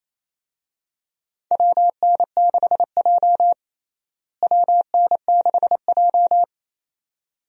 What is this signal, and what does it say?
Morse code, heard as a keyed tone, about 27 wpm, 700 Hz. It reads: WN6J WN6J
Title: USA Call Signs - 4 Characters - Repeated 2x - 28wpm